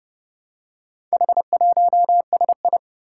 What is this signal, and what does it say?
Morse code, heard as a keyed tone, about 28 wpm, 700 Hz. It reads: H1SI
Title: Four Challenging Letter-Number Combinations 30wpm